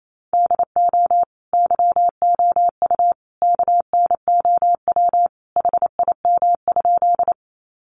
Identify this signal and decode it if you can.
Morse code, heard as a keyed tone, about 28 wpm, 700 Hz. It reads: DO YOU KNOW HIM?
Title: Sentences from Top 100 Words 28wpm